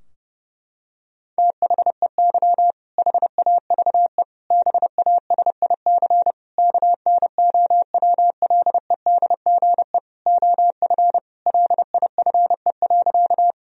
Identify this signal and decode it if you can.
Morse code, heard as a keyed tone, about 30 wpm, 700 Hz. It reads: THEY HAVE BASIC KNOWLEDGE OF LIFE.
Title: Sentences from Top 1000 Words - Review Part 3 - 30wpm